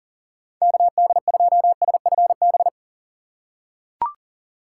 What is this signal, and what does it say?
Morse code, heard as a keyed tone, about 40 wpm, 700 Hz. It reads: KD2SFB
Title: USA Call Signs - 6 Characters - 6x Character Spacing - 40wpm